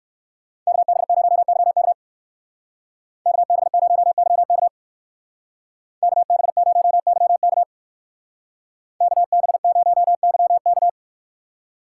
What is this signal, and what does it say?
Morse code, heard as a keyed tone, about 59 wpm, 700 Hz. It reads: KB0YK KB0YK KB0YK KB0YK